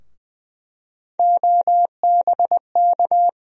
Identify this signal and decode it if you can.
Morse code, heard as a keyed tone, about 20 wpm, 700 Hz. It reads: OBK
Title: USA Call Sign Postfixes - 3 Letters 20wpm